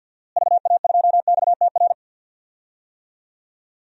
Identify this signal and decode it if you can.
Morse code, heard as a keyed tone, about 47 wpm, 700 Hz. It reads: VA2XTR
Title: DXCC-100 Call Signs - 6 Characters - Repeated 3x - 50wpm